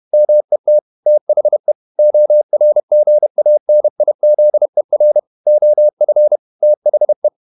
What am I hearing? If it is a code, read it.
Morse code, heard as a keyed tone, about 31 wpm, 600 Hz. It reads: MET THE ORGANIZER OF THE
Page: Never 31wpm | CW med Gnister